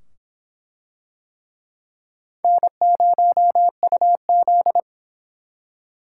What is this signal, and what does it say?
Morse code, heard as a keyed tone, about 23 wpm, 700 Hz. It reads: N0UZ